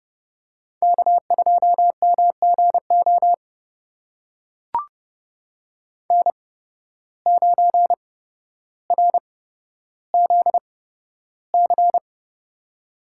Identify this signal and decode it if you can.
Morse code, heard as a keyed tone, about 31 wpm, spaced as for 13 wpm, 700 Hz. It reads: K2MGO N9RZC
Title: USA Call Signs - 5 Characters - 8x Character Spacing - 30wpm